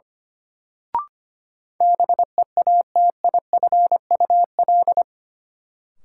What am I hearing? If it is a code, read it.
Morse code, heard as a keyed tone, about 25 wpm, 700 Hz. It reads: BEATIFUL
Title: Sentences from Top 1300 Words - New Word Drill Alt - 25wpm